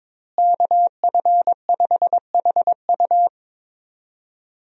Morse code, heard as a keyed tone, about 22 wpm, 700 Hz. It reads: KF5HU
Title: USA Call Signs - 5 Characters - Repeated 2x - 22wpm